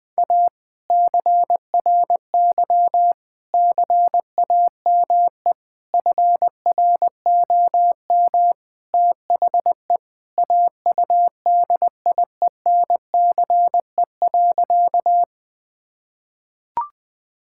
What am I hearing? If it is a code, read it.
Morse code, heard as a keyed tone, about 20 wpm, 700 Hz. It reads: A CRY CAME FROM THE AUDIENCE.
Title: Sentences from Top 1000 Words - Review Part 5 - 20wpm